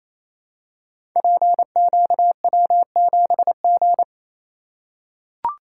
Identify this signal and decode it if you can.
Morse code, heard as a keyed tone, about 28 wpm, 700 Hz. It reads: PQW7G
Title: Code Groups - 5 Letter:Numbers - 28wpm